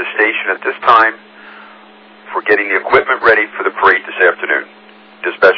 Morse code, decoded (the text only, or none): none